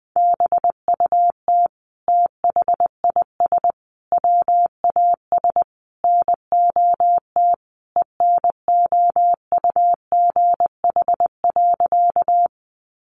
Morse code, heard as a keyed tone, about 20 wpm, 700 Hz. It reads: BUT THIS WAS NOT ENOUGH.
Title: Sentences from Top 600 Words 20wpm